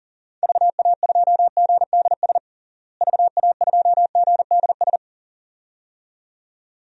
Morse code, heard as a keyed tone, about 37 wpm, 700 Hz. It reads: VA2GDS VA2GDS
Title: DXCC-100 Call Signs - 6 Characters - Repeated 2x - 40wpm